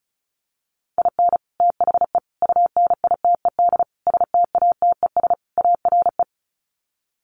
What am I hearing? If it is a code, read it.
Morse code, heard as a keyed tone, about 34 wpm, 700 Hz. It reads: IN THE UNITED STATES ARE
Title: Sets of 5 Words Encore 35wpm